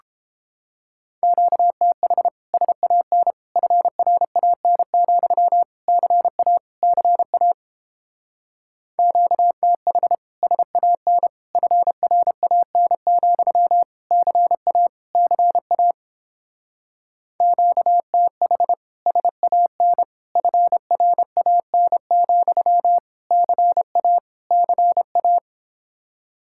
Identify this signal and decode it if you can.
Morse code, heard as a keyed tone, about 31 wpm, 700 Hz. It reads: QTH SAN FRAN, CA CA QTH SAN FRAN, CA CA QTH SAN FRAN, CA CA